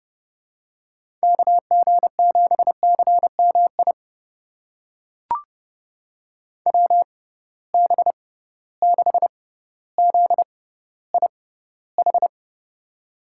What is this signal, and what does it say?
Morse code, heard as a keyed tone, about 30 wpm, spaced as for 15 wpm, 700 Hz. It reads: KG7CMI WB6ZIH
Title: USA Call Signs - 6 Characters - 6x Character Spacing - 30wpm